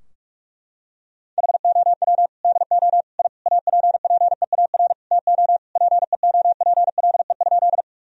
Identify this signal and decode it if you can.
Morse code, heard as a keyed tone, about 42 wpm, 700 Hz. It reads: HOW DO I APPEAR TO PEOPLE?
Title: Sentences from Top 1000 Words - Review Part 2 - 45wpm